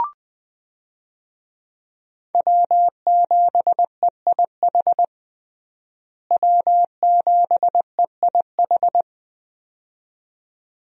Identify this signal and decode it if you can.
Morse code, heard as a keyed tone, about 20 wpm, 700 Hz. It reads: W7EIH W7EIH
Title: USA Call Signs - 5 Characters - Repeated 2x - 20wpm